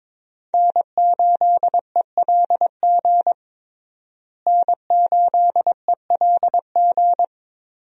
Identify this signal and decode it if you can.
Morse code, heard as a keyed tone, about 22 wpm, 700 Hz. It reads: N8ELG N8ELG
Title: USA Call Signs - 5 Characters - Repeated 3x - 22wpm